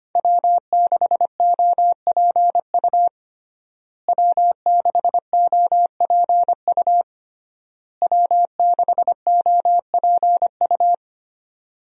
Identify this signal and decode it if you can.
Morse code, heard as a keyed tone, about 25 wpm, 700 Hz. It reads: W6OPU W6OPU W6OPU
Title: USA Call Signs - 5 Characters - Repeated 3x - 25wpm